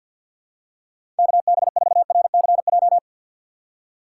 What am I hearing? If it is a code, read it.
Morse code, heard as a keyed tone, about 53 wpm, 700 Hz. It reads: KB4RCJ
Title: USA Call Signs - 4x Character Spacing - 50wpm